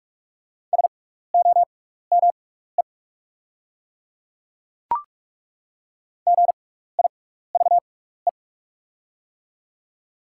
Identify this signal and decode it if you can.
Morse code, heard as a keyed tone, about 43 wpm, spaced as for 15 wpm, 690 Hz. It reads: SOME GIVE
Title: 6x Character Spacing - Top 200 Words - 45wpm